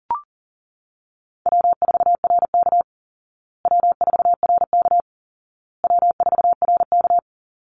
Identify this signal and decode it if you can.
Morse code, heard as a keyed tone, about 40 wpm, 700 Hz. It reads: W4RK W4RK W4RK
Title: Call Signs Repeated 2x - 40wpm